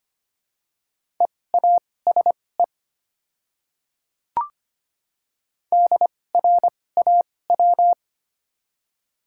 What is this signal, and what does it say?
Morse code, heard as a keyed tone, about 25 wpm, spaced as for 18 wpm, 700 Hz. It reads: EASE DRAW
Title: 2x Character Spacing - Top 300 Words - 25wpm